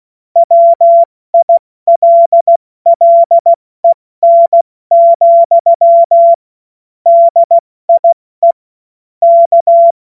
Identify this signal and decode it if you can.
Morse code, heard as a keyed tone, about 16 wpm, 680 Hz. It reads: WILLEN, DIE K
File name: Morsekurs / Witze / Witz 07. Februar 2023 70 F80